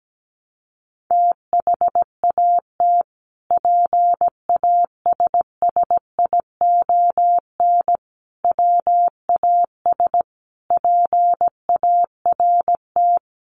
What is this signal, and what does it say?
Morse code, heard as a keyed tone, about 17 wpm, 700 Hz. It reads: THAT PASSION WAS PART
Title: Sentences from Top 1300 Words 17wpm